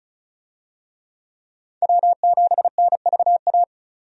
Morse code, heard as a keyed tone, about 36 wpm, 700 Hz. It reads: W7NVA
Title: USA Call Signs - 5 Characters - 35wpm